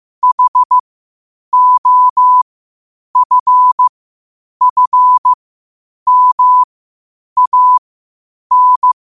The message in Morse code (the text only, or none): HOFFMAN